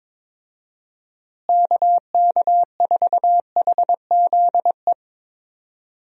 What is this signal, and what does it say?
Morse code, heard as a keyed tone, about 22 wpm, 700 Hz. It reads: KK4HZE
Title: USA Call Signs - 6 Characters - Repeated 2x - 22wpm